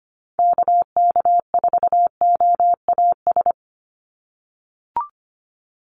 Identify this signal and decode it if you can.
Morse code, heard as a keyed tone, about 25 wpm, 690 Hz. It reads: KK4OAS
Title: USA Call Signs 25wpm